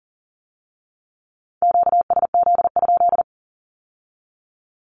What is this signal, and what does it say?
Morse code, heard as a keyed tone, about 42 wpm, 700 Hz. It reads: QSZ?
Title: Q-Codes with Spoken Meaning - 40wpm